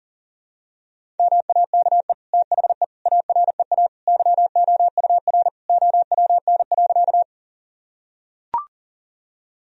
Morse code, heard as a keyed tone, about 37 wpm, 700 Hz. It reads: MAKE THE AREA YOUR OWN.
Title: Sentences from Top 1000 Words - Review Part 2 - 40wpm